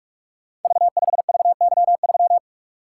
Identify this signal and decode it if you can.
Morse code, heard as a keyed tone, about 44 wpm, 700 Hz. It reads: V5VY3
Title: Code Groups - 5 Letter:Numbers - 45wpm